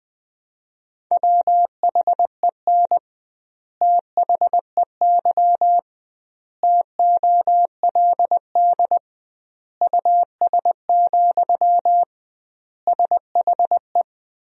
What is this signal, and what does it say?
Morse code, heard as a keyed tone, about 21 wpm, 700 Hz. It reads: WHEN THEY TOLD US, SHE
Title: 2x Word-Spacing - Sentences from Top 400 Words - 20wpm